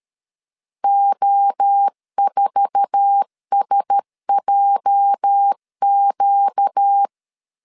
Morse code, heard as a keyed tone, about 13 wpm, 800 Hz. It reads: O4SJQ
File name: CW / DET / 1 9 1500C S B 12wpm 3819